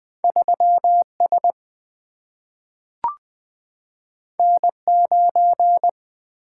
Morse code, heard as a keyed tone, about 20 wpm, 700 Hz. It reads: 3 S N 9
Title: Single-Letter Single-Number 20wpm